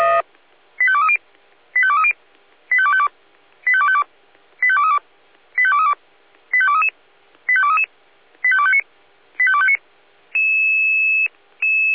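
Five-tone selective call (ZVEI), heard as quick runs of five tones, tones of about 70 ms. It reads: 85230 85230 85252 85252 85232 85232 85233 85233 85259 85259
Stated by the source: ZVEI five-tone selective call received on SDR.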